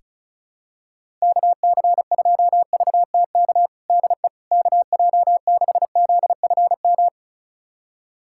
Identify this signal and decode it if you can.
Morse code, heard as a keyed tone, about 36 wpm, 700 Hz. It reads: KC2VTK DE KJ6ZFM
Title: USA Callsign DE Callsign 35wpm